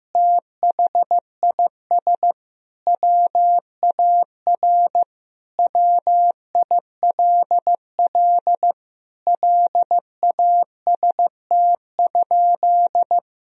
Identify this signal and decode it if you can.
Morse code, heard as a keyed tone, about 15 wpm, 700 Hz. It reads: THIS WAR WILL LAST?
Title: Sentences from Top 1000 Words - Review Part 2 - 15wpm